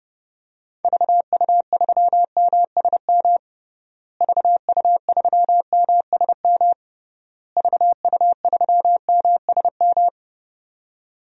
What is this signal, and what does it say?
Morse code, heard as a keyed tone, about 30 wpm, 700 Hz. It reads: VU3MSM VU3MSM VU3MSM
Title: DXCC-100 Call Signs - 6 Characters - Repeated 3x - 30wpm